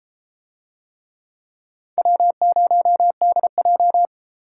Morse code, heard as a keyed tone, about 30 wpm, 700 Hz. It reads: W0DJ